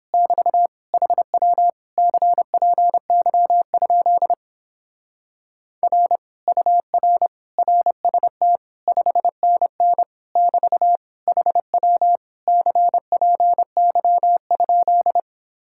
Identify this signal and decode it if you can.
Morse code, heard as a keyed tone, about 29 wpm, 700 Hz. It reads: <BT> HW CPY? R UR RST 5NN <BT> HW CPY?